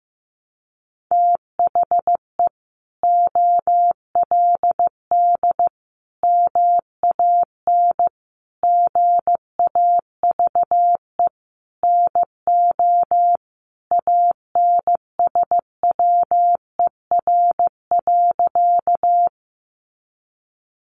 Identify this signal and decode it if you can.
Morse code, heard as a keyed tone, about 15 wpm, 700 Hz. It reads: THE OLD MAN GAVE NO ANSWER.
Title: Sentences from Top 500 Words 15wpm